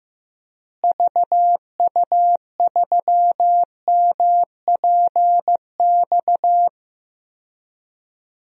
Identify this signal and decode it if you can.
Morse code, heard as a keyed tone, about 15 wpm, 700 Hz. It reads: VU3MPX
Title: DXCC-100 Call Signs - 6 Characters - Repeated 2x - 15wpm